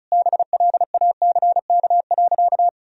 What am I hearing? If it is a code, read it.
Morse code, heard as a keyed tone, about 36 wpm, 700 Hz. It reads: BLACK.
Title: Sentences from Top 1000 Words - Review Part 2 - 35wpm